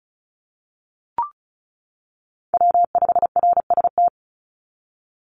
Morse code, heard as a keyed tone, about 34 wpm, 700 Hz. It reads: W5RST
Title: NA SOTA Chaser Callsigns 35wpm